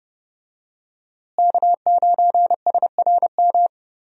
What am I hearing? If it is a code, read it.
Morse code, heard as a keyed tone, about 30 wpm, 700 Hz. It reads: K9SRM
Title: USA Call Signs - 5 Characters - Repeated 2x - 30wpm